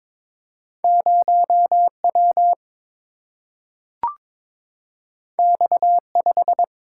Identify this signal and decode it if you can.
Morse code, heard as a keyed tone, about 22 wpm, 700 Hz. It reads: 0 W X 5
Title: Single-Letter Single-Number 22wpm